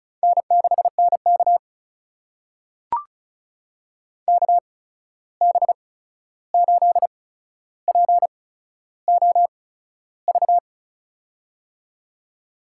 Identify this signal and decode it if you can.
Morse code, heard as a keyed tone, about 36 wpm, spaced as for 13 wpm, 700 Hz. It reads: N6NK KB8POV
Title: USA Call Signs - 8x Character Spacing - 35wpm